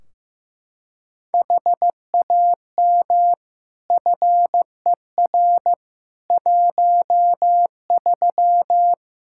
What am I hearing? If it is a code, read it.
Morse code, heard as a keyed tone, about 15 wpm, 700 Hz. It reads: HAM FER 13
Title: Practice QSO - Exchange 3 - More Info - 15wpm